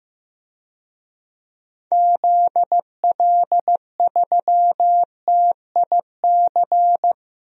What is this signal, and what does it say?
Morse code, heard as a keyed tone, about 15 wpm, 700 Hz. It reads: ZL3TIC
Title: DXCC-100 Call Signs - 6 Characters - Repeated 2x - 15wpm